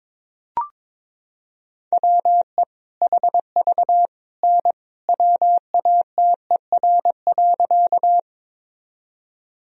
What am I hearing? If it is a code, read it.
Morse code, heard as a keyed tone, about 22 wpm, 700 Hz. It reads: WE HV N WATER.
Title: Sentences from Top 100 Words and Abbreviations 22wpm